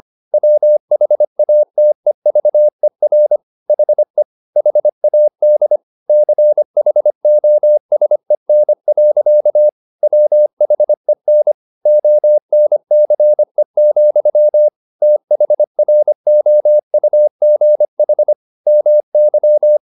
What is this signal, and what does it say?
Morse code, heard as a keyed tone, about 25 wpm, 600 Hz. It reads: WHATEVER HE HAD CHOSEN. WHEN ONCE, THROUGH MY
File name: the old man and the sea part 1 25wpm